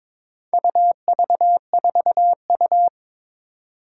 Morse code, heard as a keyed tone, about 22 wpm, 700 Hz. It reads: UV4U
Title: Four Challenging Letter-Number Combinations 22wpm